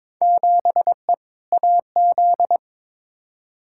Morse code, heard as a keyed tone, about 22 wpm, 700 Hz. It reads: 7E AZ
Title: Field Day Exchanges Encore - 22wpm